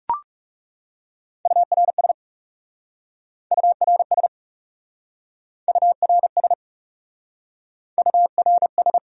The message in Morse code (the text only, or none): URS URS URS URS